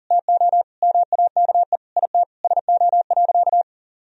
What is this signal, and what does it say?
Morse code, heard as a keyed tone, about 40 wpm, 700 Hz. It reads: TO MAKE IT SO.
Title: Sentences from Top 1000 Words - Review Part 3 - 40wpm